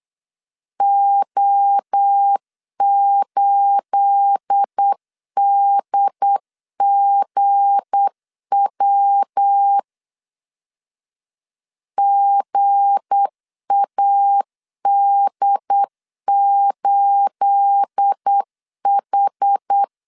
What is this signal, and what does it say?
Morse code, heard as a keyed tone, about 8 wpm, 800 Hz. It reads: O8DGW GAD8H
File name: CW / DET / 1 9 1500C S C 8wpm 5653